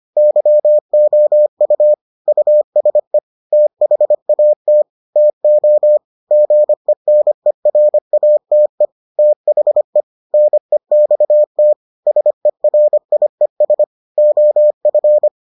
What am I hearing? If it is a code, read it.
Morse code, heard as a keyed tone, about 25 wpm, 600 Hz. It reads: YOU USE THAT TO GENERATE THE NEXT SERIES OF